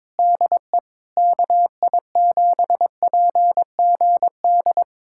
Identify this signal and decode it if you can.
Morse code, heard as a keyed tone, about 22 wpm, 700 Hz. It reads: DE KI7PGD
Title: USA Callsign DE Callsign 22wpm